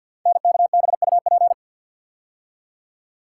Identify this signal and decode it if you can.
Morse code, heard as a keyed tone, about 52 wpm, 700 Hz. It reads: NKBUP
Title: Code Groups - 5 Letters - 50wpm